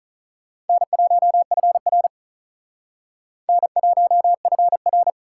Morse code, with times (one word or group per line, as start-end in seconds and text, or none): 0.7-2.1 N1FR
3.5-5.1 N1FR